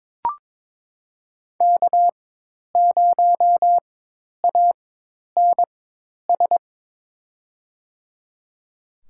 Morse code, heard as a keyed tone, about 21 wpm, spaced as for 10 wpm, 700 Hz. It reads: K0ANS
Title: USA Call Signs - 5 Characters - 4x Character Spacing - 22wpm